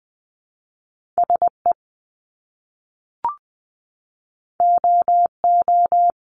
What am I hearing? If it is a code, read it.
Morse code, heard as a keyed tone, about 20 wpm, 700 Hz. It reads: SE OO
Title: Top Two Letter Combinations in Words 20wpm